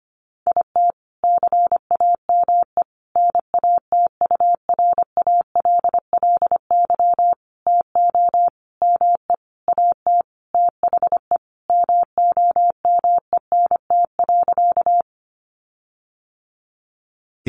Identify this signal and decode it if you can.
Morse code, heard as a keyed tone, about 25 wpm, 700 Hz. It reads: IT CAME NATURALLY TO ME AT THE MOMENT.
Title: Sentences from Top 1300 Words 25wpm